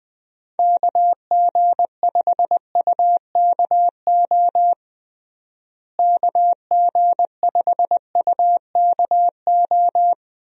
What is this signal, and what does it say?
Morse code, heard as a keyed tone, about 20 wpm, 700 Hz. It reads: KG5UKO KG5UKO
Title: USA Call Signs - 6 Characters - Repeated 3x - 20wpm